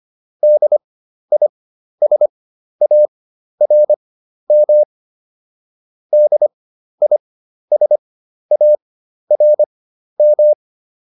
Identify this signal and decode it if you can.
Morse code, heard as a keyed tone, about 25 wpm, spaced as for 12 wpm, 600 Hz. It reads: DISARM DISARM